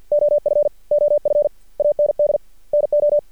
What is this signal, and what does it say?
Morse code, heard as a keyed tone, about 53 wpm, 590 Hz. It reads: OF OF AND NO
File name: archived / CRSnet